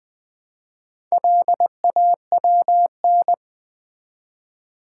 Morse code, heard as a keyed tone, about 20 wpm, 700 Hz. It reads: LAWN
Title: Build A Word - 1 Letter At A Time - 4-Letter Words - 20wpm